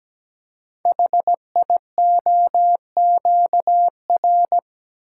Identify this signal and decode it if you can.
Morse code, heard as a keyed tone, about 17 wpm, 700 Hz. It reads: HIOQR
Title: Code Groups - 5 Letter:Numbers - 17wpm